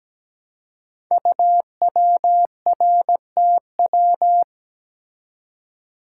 Morse code, heard as a keyed tone, about 17 wpm, 700 Hz. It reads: UWRTW
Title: Code Groups - 5 Letters - 17wpm